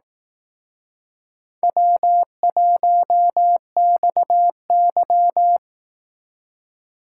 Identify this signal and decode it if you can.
Morse code, heard as a keyed tone, about 18 wpm, 700 Hz. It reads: W1XY
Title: Speed Racing - USA 1x2 Call Signs 15wpm